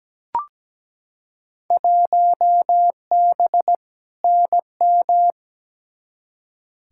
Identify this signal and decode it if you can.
Morse code, heard as a keyed tone, about 17 wpm, 700 Hz. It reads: 1B NM
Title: Field Day Exchanges Encore - 17wpm